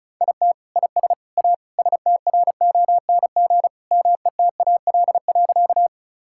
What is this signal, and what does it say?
Morse code, heard as a keyed tone, about 36 wpm, 700 Hz. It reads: IT IS A STRONG METAL.
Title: Sentences from Top 1000 Words - Review Part 6 - 35wpm